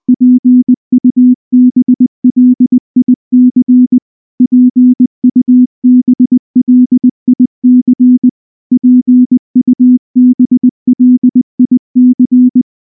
Morse code, heard as a keyed tone, about 19 wpm, 250 Hz. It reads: PUBLIC PUBLIC PUBLIC
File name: morse / morse-training-20WPM-1